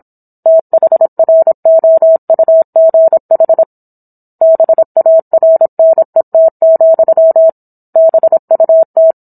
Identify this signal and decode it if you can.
Morse code, heard as a keyed tone, about 27 wpm, 650 Hz. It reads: THROUGH BARNET, BUT